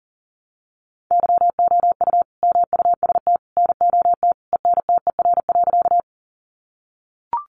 Morse code, heard as a keyed tone, about 40 wpm, 700 Hz. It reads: YOU MUST NOT ENTER.
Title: Sentences from Top 1300 Words 40wpm